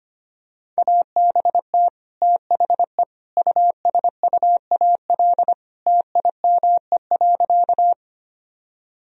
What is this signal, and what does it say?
Morse code, heard as a keyed tone, about 25 wpm, 700 Hz. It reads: ABT THE USUAL TIME.
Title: Sentences from Top 400 Words and Abbreviations 25wpm